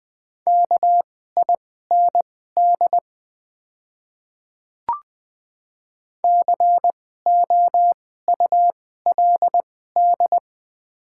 Morse code, heard as a keyed tone, about 20 wpm, spaced as for 14 wpm, 700 Hz. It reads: KIND COULD
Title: 2x Character Spacing - Top 200 Words - 20wpm